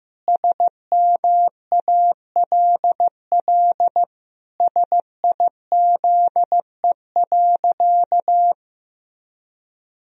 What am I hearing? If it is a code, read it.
Morse code, heard as a keyed tone, about 16 wpm, 700 Hz. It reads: SMALL SIZE.
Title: Sentences from Top 1000 Words - Review Part 2 - 15wpm